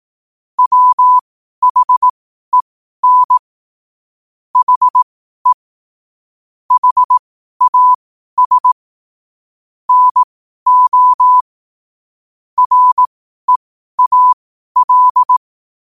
Morse code, heard as a keyed tone, about 18 wpm, spaced as for 12 wpm, 1000 Hz. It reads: WHEN HE HAS NO REAL
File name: qotdfiles / 12wpm